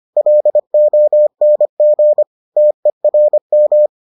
Morse code, heard as a keyed tone, about 25 wpm, 600 Hz. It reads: LONG TERM